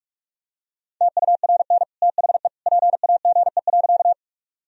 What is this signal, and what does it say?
Morse code, heard as a keyed tone, about 42 wpm, 700 Hz. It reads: TURN THE PAGE.
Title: Sentences from Top 1000 Words - Review Part 1 - 45wpm